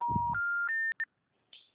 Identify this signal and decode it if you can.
Special information tone.